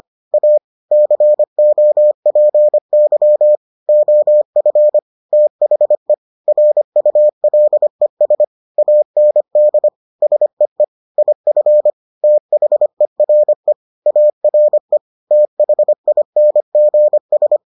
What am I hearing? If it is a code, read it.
Morse code, heard as a keyed tone, about 25 wpm, 600 Hz. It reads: A COPY OF THE RULES AND SEE IF THERE ARE THINGS